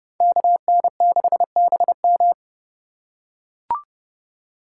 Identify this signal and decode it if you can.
Morse code, heard as a keyed tone, about 29 wpm, 700 Hz. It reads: KN6BM